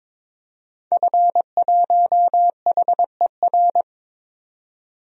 Morse code, heard as a keyed tone, about 22 wpm, 700 Hz. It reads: F1HER
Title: DXCC-100 Call Signs - 5 Characters - 22wpm